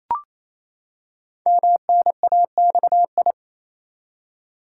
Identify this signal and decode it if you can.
Morse code, heard as a keyed tone, about 28 wpm, 700 Hz. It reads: MNAXI
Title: Code Groups - 5 Letters - 28wpm